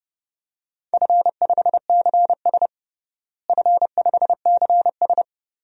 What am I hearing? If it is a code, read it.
Morse code, heard as a keyed tone, about 30 wpm, 700 Hz. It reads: F5CS F5CS
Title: DXCC-100 Call Signs - 4 Characters - Repeated 3x - 30wpm